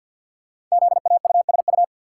Morse code, heard as a keyed tone, about 47 wpm, 700 Hz. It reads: ZAUSU
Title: Code Groups - 5 Letters - 50wpm